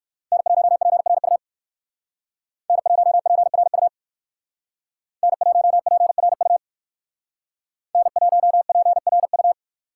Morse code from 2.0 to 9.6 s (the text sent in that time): N1PRU N1PRU N1PRU